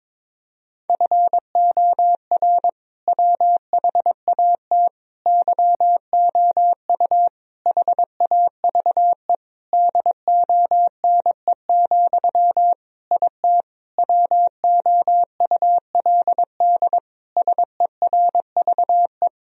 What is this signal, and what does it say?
Morse code, heard as a keyed tone, about 22 wpm, 700 Hz. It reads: FOR WHAT YOU HAVE DONE, IT WOULD SERVE
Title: Sentences from Top 1000 Words - Review Part 2 - 22wpm